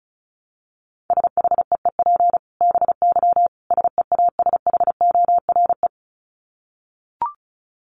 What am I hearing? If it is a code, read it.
Morse code, heard as a keyed tone, about 34 wpm, 700 Hz. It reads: SHEEP BY SEASHORE
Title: Dit Challenge 35wpm